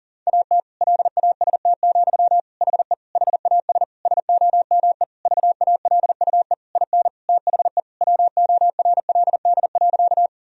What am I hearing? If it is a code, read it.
Morse code, heard as a keyed tone, about 40 wpm, 700 Hz. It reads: AT LAST, HE HAS SOME VALUE IN THE WORLD.
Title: Sentences from Top 1000 Words - Review Part 3 - 40wpm